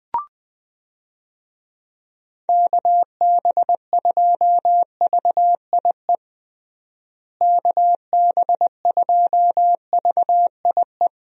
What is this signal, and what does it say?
Morse code, heard as a keyed tone, about 21 wpm, 700 Hz. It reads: KB2VIE KB2VIE
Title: USA Call Signs - 6 Characters - Repeated 2x - 20wpm